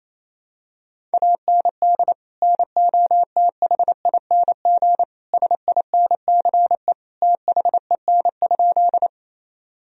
Morse code, heard as a keyed tone, about 28 wpm, 700 Hz. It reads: AND NOTHING SINCE THEN?
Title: Sentences from Top 1000 Words - Review Part 1 - 28wpm